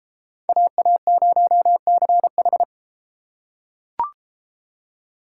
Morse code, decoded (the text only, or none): AA0CH